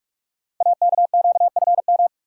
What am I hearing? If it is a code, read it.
Morse code, heard as a keyed tone, about 44 wpm, 690 Hz. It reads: AKQFM
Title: Code Groups - 5 Letter:Numbers - 45wpm